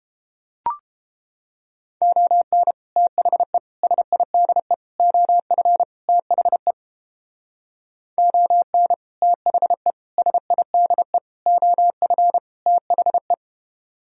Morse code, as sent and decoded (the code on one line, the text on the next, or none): --- -. / - .... / ... .. -.. / --- ..-. / - .... / --- -. / - .... / ... .. -.. / --- ..-. / - .... .
ON THE SIDE OF THE ON THE SIDE OF THE